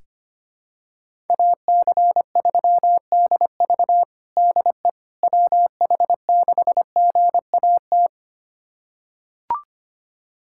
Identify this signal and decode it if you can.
Morse code, heard as a keyed tone, about 25 wpm, 700 Hz. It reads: AC3DV DE WH6GAT
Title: USA Callsign DE Callsign 25wpm